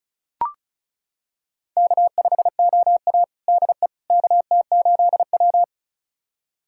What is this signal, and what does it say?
Morse code, heard as a keyed tone, about 36 wpm, 700 Hz. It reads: K5OA DE KT8W
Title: USA Callsign DE Callsign - 4 Characters - 35wpm